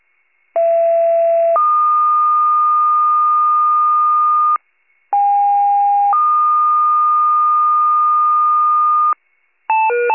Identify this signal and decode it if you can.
Two-tone page, heard as a one-second tone, then a three-second tone, about 670 then 1220 Hz, about 810 then 1220 Hz. It is heard from 0.5 to 9.1 s.